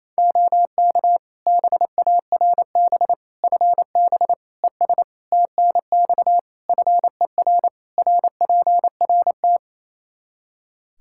Morse code, heard as a keyed tone, about 28 wpm, 700 Hz. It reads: OK BARB FB ES TNX FER RPRT
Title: Practice QSO - Exchange 2 - Info - 28wpm